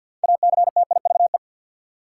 Morse code, heard as a keyed tone, about 53 wpm, 700 Hz. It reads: ACTIVE
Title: Sentences from Top 900 Words - New Word Drill - 50wpm